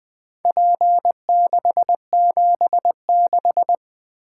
Morse code, heard as a keyed tone, about 20 wpm, 700 Hz. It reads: P676
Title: Four Letter-Number Combinations - 20wpm